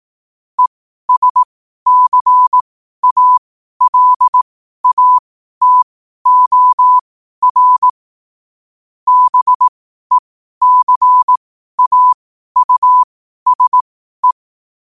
Morse code, heard as a keyed tone, about 18 wpm, spaced as for 12 wpm, 1000 Hz. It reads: ESCALATOR BECAUSE
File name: qotdfiles / 12wpm